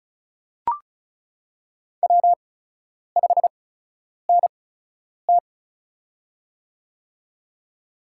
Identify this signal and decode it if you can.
Morse code, heard as a keyed tone, about 36 wpm, spaced as for 10 wpm, 700 Hz. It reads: W5NT
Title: USA Call Signs - 4 Characters - 8x Character Spacing - 35wpm